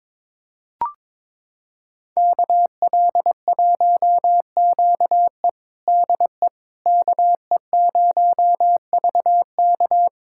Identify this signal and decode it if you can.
Morse code, heard as a keyed tone, about 22 wpm, 700 Hz. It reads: KL1QE DE KE0VK
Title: USA Callsign DE Callsign - 5 Characters - 22wpm